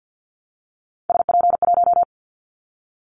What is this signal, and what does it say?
Morse code, heard as a keyed tone, about 50 wpm, 700 Hz. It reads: SP1
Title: Callsign Prefixes for Top 25 DXCC - 50wpm